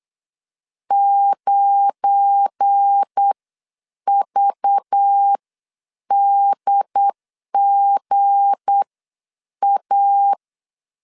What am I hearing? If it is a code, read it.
Morse code, heard as a keyed tone, about 8 wpm, spaced as for 7 wpm, 800 Hz. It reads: 9VDGA
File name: CW / DET / 1 9 1500C S A 8wpm 5637